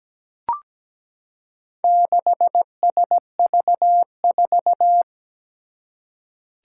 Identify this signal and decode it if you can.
Morse code, heard as a keyed tone, about 17 wpm, 700 Hz. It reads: 6SV4
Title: Four Challenging Letter-Number Combinations 17wpm